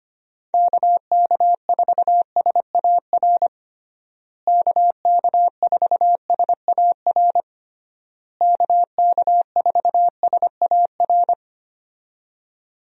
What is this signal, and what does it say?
Morse code, heard as a keyed tone, about 25 wpm, 700 Hz. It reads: KK4SAR KK4SAR KK4SAR
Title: USA Call Signs - 6 Characters - Repeated 3x - 25wpm